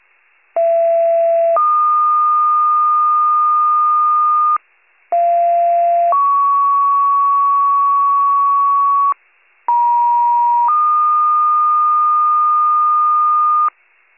Two-tone page, heard as a one-second tone, then a three-second tone, about 670 then 1220 Hz, about 690 then 1120 Hz, about 950 then 1220 Hz. It is heard from 0.6 to 13.7 s.